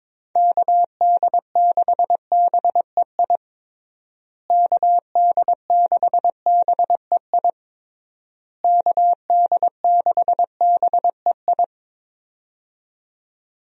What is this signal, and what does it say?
Morse code, heard as a keyed tone, about 22 wpm, 700 Hz. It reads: KD6BEI KD6BEI KD6BEI
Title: USA Call Signs - 6 Characters - Repeated 3x - 22wpm